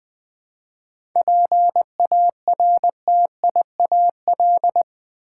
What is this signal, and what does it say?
Morse code, heard as a keyed tone, about 20 wpm, 700 Hz. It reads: PARTIAL
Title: 7 Letter Words - First 3 Letters Given - 20wpm